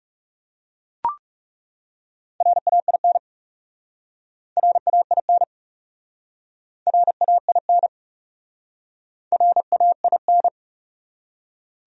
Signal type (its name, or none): Morse code